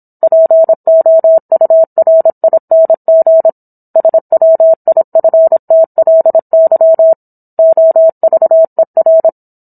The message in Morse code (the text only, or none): POURING SWIFTLY OVER